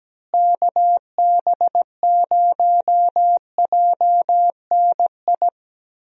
Morse code, heard as a keyed tone, about 17 wpm, 700 Hz. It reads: KB0JNI
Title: USA Call Signs - 6 Characters - Repeated 2x - 17wpm